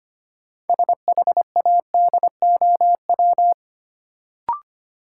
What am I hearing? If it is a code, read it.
Morse code, heard as a keyed tone, about 25 wpm, 700 Hz. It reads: SHADOW
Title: Sets of 5 Words Encore - New Word Drill - 25wpm